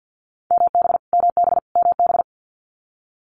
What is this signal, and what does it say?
Morse code, heard as a keyed tone, about 50 wpm, 700 Hz. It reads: MB MB MB
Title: Multi-letter Phonemes Repeated 3x 50wpm